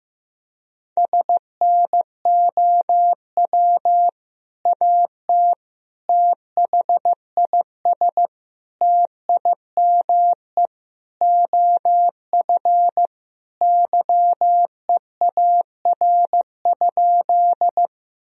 Morse code, heard as a keyed tone, about 14 wpm, 700 Hz. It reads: SNOW AT THIS TIME OF YEAR?
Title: Sentences from Top 1000 Words - Review Part 3 - 15wpm